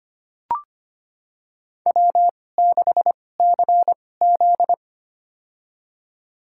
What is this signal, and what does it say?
Morse code, heard as a keyed tone, about 25 wpm, spaced as for 18 wpm, 700 Hz. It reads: W6CZ
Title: USA Call Signs - 4 Characters - 2x Character Spacing - 25wpm